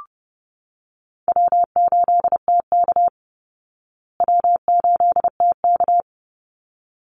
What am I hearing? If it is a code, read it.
Morse code, heard as a keyed tone, about 30 wpm, 700 Hz. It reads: W8TK W8TK
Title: Easy Call Signs Repeated 2x - 30wpm